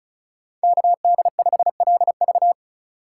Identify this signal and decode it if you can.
Morse code, heard as a keyed tone, about 34 wpm, 700 Hz. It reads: KD5LV
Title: USA Call Signs - 5 Characters - 2x Character Spacing - 35wpm